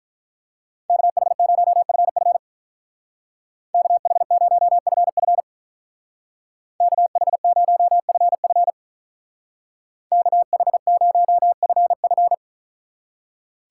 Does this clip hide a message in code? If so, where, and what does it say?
Yes: Morse code from 0.9 to 12.3 s, heard as a keyed tone, about 44 wpm, 700 Hz. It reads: KH0FF KH0FF KH0FF KH0FF